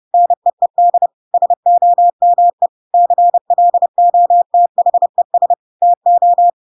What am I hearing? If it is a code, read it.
Morse code, heard as a keyed tone, about 30 wpm, 700 Hz. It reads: NEED SOME CLOTHES TO
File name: Pinocchio / Pinocchio ch8 30 30 ik1hge